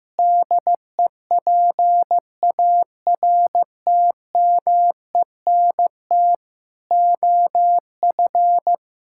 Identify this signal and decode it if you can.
Morse code, heard as a keyed tone, about 16 wpm, 700 Hz. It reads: DEPARTMENT OF
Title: Sentences from Top 1000 Words - Review Part 5 - 15wpm